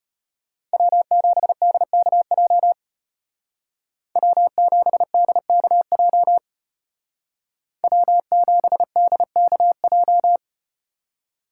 Morse code, heard as a keyed tone, about 33 wpm, 700 Hz. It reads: W7DKJ W7DKJ W7DKJ